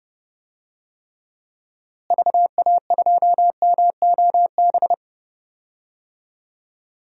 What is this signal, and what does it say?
Morse code, heard as a keyed tone, about 30 wpm, 700 Hz. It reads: VA2MOB
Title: DXCC-100 Call Signs - 6 Characters - Repeated 3x - 30wpm